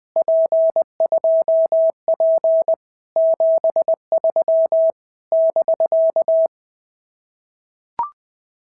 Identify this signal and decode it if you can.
Morse code, heard as a keyed tone, about 20 wpm, 650 Hz. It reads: P2P 73 <BK>
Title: POTA QSO Practice - 20wpm